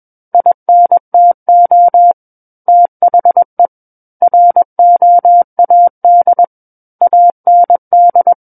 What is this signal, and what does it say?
Morse code, heard as a keyed tone, about 21 wpm, 700 Hz. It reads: INTO THE ROAD AND
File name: Book One / 12-Chapter 12 - 21 WPM